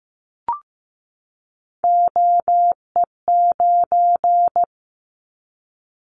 Morse code, heard as a keyed tone, about 15 wpm, 700 Hz. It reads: OE9
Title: Callsign Prefixes for Top 50 DXCC - 15wpm